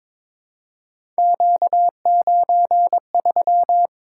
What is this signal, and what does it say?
Morse code, heard as a keyed tone, about 22 wpm, 700 Hz. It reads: Q93
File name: practice / 22wpm / Three Letter-Number Combinations - 22wpm